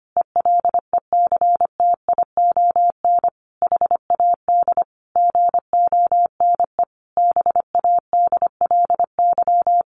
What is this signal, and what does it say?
Morse code, heard as a keyed tone, about 25 wpm, 700 Hz. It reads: ELECTION HAD GONE BADLY
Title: A Sound Of Thunder 25wpm